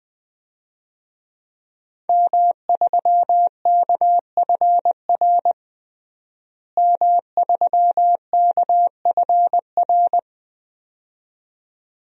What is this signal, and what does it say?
Morse code, heard as a keyed tone, about 20 wpm, 700 Hz. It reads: M3KFR M3KFR
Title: DXCC-100 Call Signs - 5 Characters - Repeated 2x - 20wpm